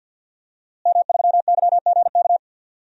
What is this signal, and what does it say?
Morse code, heard as a keyed tone, about 51 wpm, 700 Hz. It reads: M3YGK
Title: DXCC-100 Call Signs - 5 Characters - Repeated 2x - 50wpm